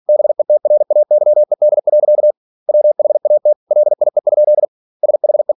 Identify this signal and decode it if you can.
Morse code, heard as a keyed tone, about 46 wpm, 600 Hz. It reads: BETRAYED. WHAT LIE? SHE
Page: Never 47wpm